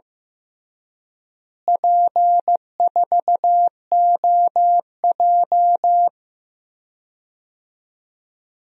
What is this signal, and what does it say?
Morse code, heard as a keyed tone, about 15 wpm, 700 Hz. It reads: P4OJ
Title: DXCC-100 Call Signs - 4 Characters - Repeated 2x - 15wpm